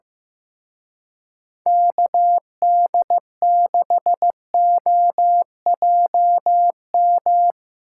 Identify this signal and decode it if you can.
Morse code, heard as a keyed tone, about 17 wpm, 700 Hz. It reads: KD6OJM
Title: Speed Racing - USA Call Signs 15wpm